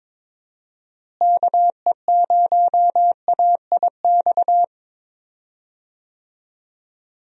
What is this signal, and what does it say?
Morse code, heard as a keyed tone, about 22 wpm, 700 Hz. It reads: KE0AIX
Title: USA Call Signs - 6 Characters - Repeated 2x - 22wpm